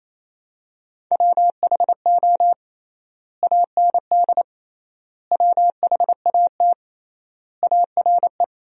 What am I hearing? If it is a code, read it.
Morse code, heard as a keyed tone, about 28 wpm, 700 Hz. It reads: WHO AND WHAT ARE
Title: 3x Word-Spacing - Sentences from Top 100 Words - 28wpm